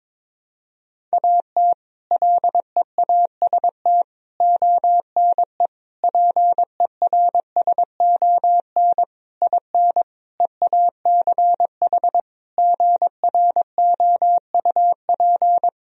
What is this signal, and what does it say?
Morse code, heard as a keyed tone, about 22 wpm, 700 Hz. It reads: AT LEAST ONE PERSON IN EACH GROUP
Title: Sentences from Top 1000 Words - Review Part 1 - 22wpm